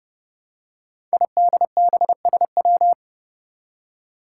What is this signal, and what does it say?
Morse code, heard as a keyed tone, about 28 wpm, 700 Hz. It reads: IDBSW
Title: Five Challenging Letter-Number Combinations 30wpm